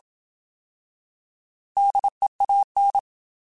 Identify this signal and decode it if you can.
Morse code, heard as a keyed tone, about 27 wpm, 800 Hz. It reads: DEAN